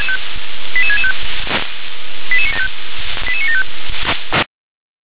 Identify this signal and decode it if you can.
Five-tone selective call (ZVEI), heard as quick runs of five tones, tones of about 70 ms. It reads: 88665 88975